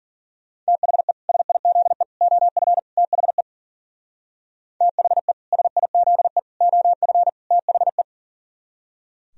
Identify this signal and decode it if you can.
Morse code, heard as a keyed tone, about 49 wpm, 700 Hz. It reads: THE SIZE OF THE THE SIZE OF THE
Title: Speed-Racing - Sets of 4 Words 40wpm